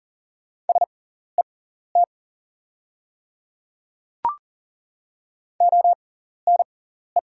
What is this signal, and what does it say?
Morse code, heard as a keyed tone, about 40 wpm, spaced as for 14 wpm, 690 Hz. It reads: SET ONE
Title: 6x Character Spacing - Top 300 Words - 40wpm